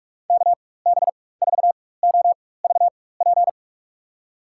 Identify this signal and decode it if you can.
Morse code, heard as a keyed tone, about 46 wpm, spaced as for 21 wpm, 700 Hz. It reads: KB4OVP
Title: USA Call Signs - 6 Characters - 4x Character Spacing - 45wpm